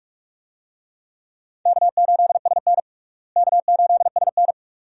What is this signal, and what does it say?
Morse code, heard as a keyed tone, about 46 wpm, 700 Hz. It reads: K8SN K8SN
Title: USA Call Signs - 4 Characters - Repeated 2x - 45wpm